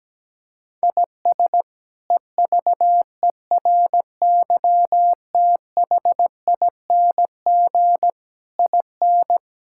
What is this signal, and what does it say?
Morse code, heard as a keyed tone, about 17 wpm, 700 Hz. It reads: IS EVERYTHING IN
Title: Sentences from Top 1000 Words - Review Part 6 - 17wpm